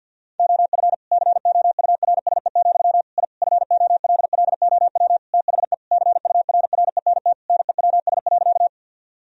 Morse code, heard as a keyed tone, about 50 wpm, 700 Hz. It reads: OF COURSE, I FOLLOW THE CURRENT NEWS.
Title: Sentences from Top 1000 Words - Review Part 4 - 50wpm